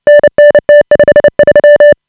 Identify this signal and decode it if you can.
Morse code, heard as a keyed tone, about 30 wpm, 590 Hz. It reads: NN T53
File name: AUDIO2018 / HA5BA CW